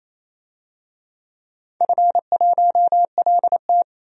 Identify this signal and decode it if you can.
Morse code, heard as a keyed tone, about 27 wpm, 700 Hz. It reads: F1LT
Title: DXCC-100 Call Signs - 4 Characters - 28wpm